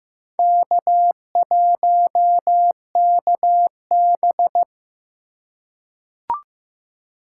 Morse code, heard as a keyed tone, about 15 wpm, 700 Hz. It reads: K1KB
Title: USA Call Signs - 4 Characters - 2x Character Spacing - 15wpm